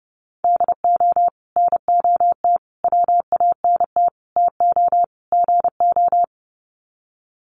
Morse code, heard as a keyed tone, about 30 wpm, 700 Hz. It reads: DO NOT WANT TO GO
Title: Sets of 5 Words 30wpm